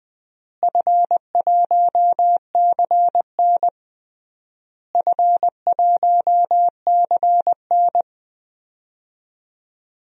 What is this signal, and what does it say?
Morse code, heard as a keyed tone, about 21 wpm, 700 Hz. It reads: F1CN F1CN
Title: DXCC-100 Call Signs - 4 Characters - Repeated 3x - 20wpm